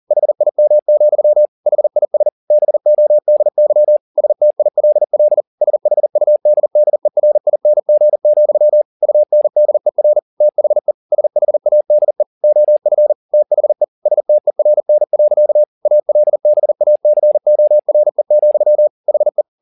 Morse code, heard as a keyed tone, about 40 wpm, 600 Hz. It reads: HIM, HIS BODY STILL SHUDDERING, UNDER THE SHADE OF THE STERN. ALBACORE, HE